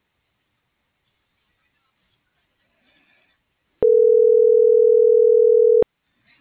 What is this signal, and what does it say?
Ringback tone (North America), 440 + 480 Hz.